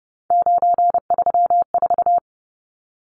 Morse code, heard as a keyed tone, about 30 wpm, 700 Hz. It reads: 934
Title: Numbers - 3-Digit 30wpm